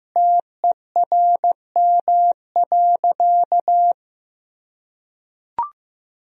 Morse code, heard as a keyed tone, about 16 wpm, 700 Hz. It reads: TERM.
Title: Sentences from Top 1000 Words - Review Part 4 - 15wpm